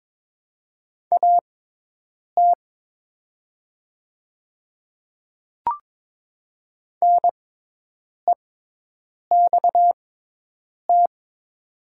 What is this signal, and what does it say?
Morse code, heard as a keyed tone, about 22 wpm, spaced as for 8 wpm, 700 Hz. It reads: AT NEXT
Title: 6x Character Spacing - Top 500 Words - 22wpm